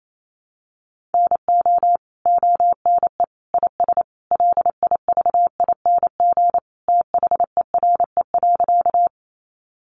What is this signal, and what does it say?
Morse code, heard as a keyed tone, about 28 wpm, 700 Hz. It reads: NO ONE IS LIVING THERE.
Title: Sentences from Top 1100 Words 28wpm